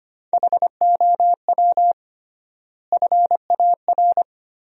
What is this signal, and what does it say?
Morse code, heard as a keyed tone, about 25 wpm, 700 Hz. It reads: HOW FAR
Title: 3x Word-Spacing - Sentences from Top 400 Words - 25wpm